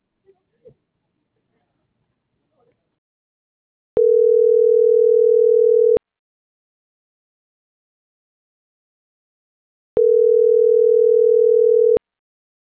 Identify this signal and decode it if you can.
Ringback tone (North America), 440 + 480 Hz.